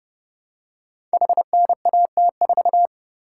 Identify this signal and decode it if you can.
Morse code, heard as a keyed tone, about 29 wpm, 700 Hz. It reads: HNAT4
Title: Code Groups - 5 Letter:Numbers - 30wpm